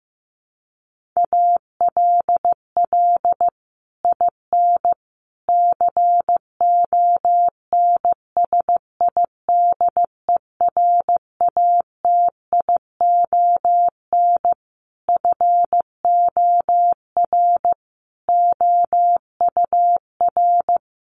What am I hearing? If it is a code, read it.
Morse code, heard as a keyed tone, about 15 wpm, 700 Hz. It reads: ALL IN CONSIDERATION FOR OUR
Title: Sentences from Top 1500 Words 15wpm